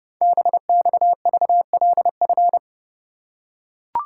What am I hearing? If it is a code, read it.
Morse code, heard as a keyed tone, about 30 wpm, 700 Hz. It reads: BXVLF
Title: Code Groups - 5 Letters - 30wpm